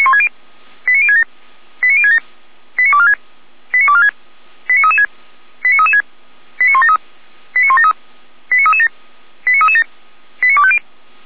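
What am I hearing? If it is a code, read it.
Five-tone selective call (ZVEI), heard as quick runs of five tones, tones of about 70 ms. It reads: 89067 89067 89246 89246 89336 89336 89173 89173 89337 89337 89250
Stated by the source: ZVEI five-tone selective call received by SDR, from the feuerwehr-orlamuende.org fire brigade audio server.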